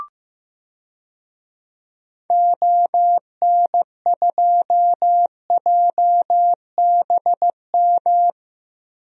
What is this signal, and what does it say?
Morse code, heard as a keyed tone, about 15 wpm, 700 Hz. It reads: ON2JBM
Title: DXCC-100 Call Signs - 6 Characters - 15wpm